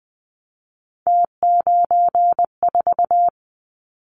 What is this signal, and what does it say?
Morse code, heard as a keyed tone, about 20 wpm, 700 Hz. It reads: T94
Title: Callsign Prefixes for Top 50 DXCC - 20wpm